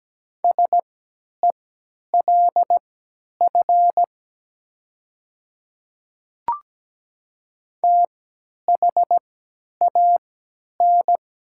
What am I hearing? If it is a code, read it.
Morse code, heard as a keyed tone, about 17 wpm, spaced as for 10 wpm, 700 Hz. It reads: SELF THAN
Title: 3x Character Spacing - Top 200 Words - 17wpm